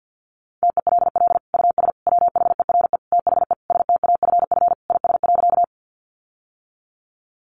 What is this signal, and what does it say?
Morse code, heard as a keyed tone, about 50 wpm, 700 Hz. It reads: TELL US WHERE THE STAFF IS.
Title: Sentences from Top 1200 Words 50wpm